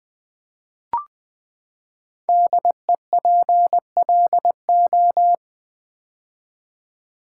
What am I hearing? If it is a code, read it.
Morse code, heard as a keyed tone, about 20 wpm, 700 Hz. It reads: DEPLO
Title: Build A Word - 5 Letters At A Time - 10-Letter Words - 20wpm